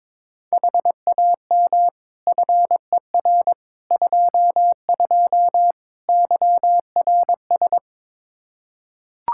Morse code, heard as a keyed tone, about 22 wpm, 700 Hz. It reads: HAM FER 22 YRS
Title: Practice QSO - Exchange 3 - More Info - 22wpm